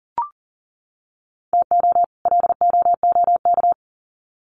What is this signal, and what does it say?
Morse code, heard as a keyed tone, about 40 wpm, 700 Hz. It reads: TO LOOK
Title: Sets of 2 Words 40wpm